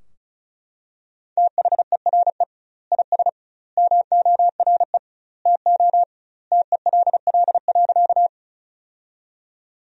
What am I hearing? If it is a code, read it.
Morse code, heard as a keyed tone, about 36 wpm, 700 Hz. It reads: THERE IS MORE TO TELL.
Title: 2x Word-Spacing - Sentences from Top 200 Words - 35wpm